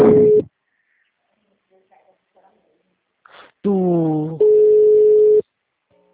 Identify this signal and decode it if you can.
Ringback tone (Europe, 425 Hz), with speech.